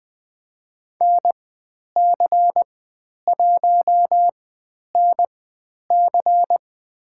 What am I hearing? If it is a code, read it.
Morse code, heard as a keyed tone, about 20 wpm, spaced as for 10 wpm, 700 Hz. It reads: NC1NC
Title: 095 Lesson 16g - Letter c - Call Signs 20wpm